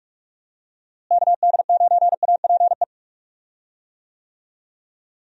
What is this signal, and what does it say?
Morse code, heard as a keyed tone, about 46 wpm, 700 Hz. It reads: KD9APE
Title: USA Call Signs - 6 Characters - Repeated 2x - 45wpm